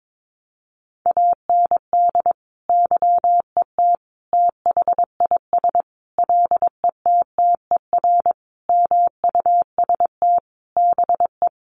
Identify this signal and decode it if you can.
Morse code, heard as a keyed tone, about 22 wpm, 700 Hz. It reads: AND YET THIS LETTER MUST BE
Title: Sentences from Top 1400 Words 22wpm